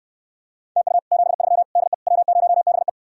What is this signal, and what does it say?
Morse code, heard as a keyed tone, about 69 wpm, 700 Hz. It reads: TU 73 DE W0BE